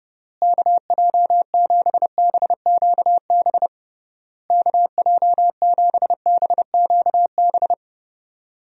Morse code, heard as a keyed tone, about 30 wpm, 700 Hz. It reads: KJ7BQB KJ7BQB
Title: USA Call Signs - 6 Characters - Repeated 3x - 30wpm